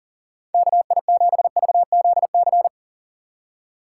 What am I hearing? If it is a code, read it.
Morse code, heard as a keyed tone, about 40 wpm, 700 Hz. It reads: KI7VZC
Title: USA Call Signs - 6 Characters - 2x Character Spacing - 40wpm